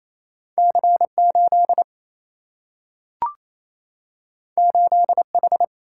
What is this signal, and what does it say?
Morse code, heard as a keyed tone, about 28 wpm, 700 Hz. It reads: C8 8H
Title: Single-Letter Single-Number 28wpm